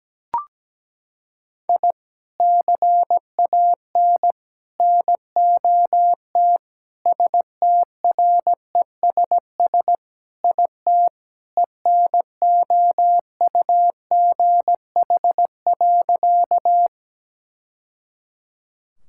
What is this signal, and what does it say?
Morse code, heard as a keyed tone, about 17 wpm, 700 Hz. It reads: I CAN NOT STRESS IT ENOUGH.
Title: Sentences from Top 1000 Words - Review Part 6 - 17wpm